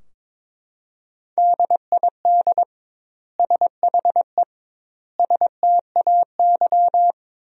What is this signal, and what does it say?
Morse code, heard as a keyed tone, about 22 wpm, 700 Hz. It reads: DID SHE STAY
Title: 2x Word-Spacing - Sentences from Top 500 Words - 22wpm